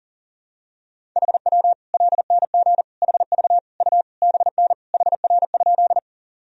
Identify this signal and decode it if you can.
Morse code, heard as a keyed tone, about 40 wpm, 700 Hz. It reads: HW LNG HV U BN HR?
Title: Sentences from Top 200 Words and Abbreviations 40wpm